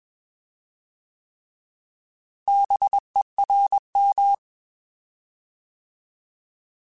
Morse code, heard as a keyed tone, about 21 wpm, 800 Hz. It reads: BERM
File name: Alphabetical / Alphabetical mp3s / Bs @20